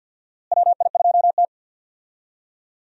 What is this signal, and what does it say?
Morse code, heard as a keyed tone, about 54 wpm, 700 Hz. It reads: WI2T
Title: USA Call Signs - 4 Characters - Repeated 3x - 50wpm